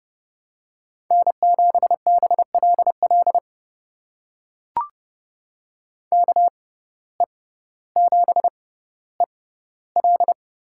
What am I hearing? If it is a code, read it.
Morse code, heard as a keyed tone, about 30 wpm, spaced as for 15 wpm, 700 Hz. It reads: N7BLL KE7EL
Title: USA Call Signs - 5 Characters - 6x Character Spacing - 30wpm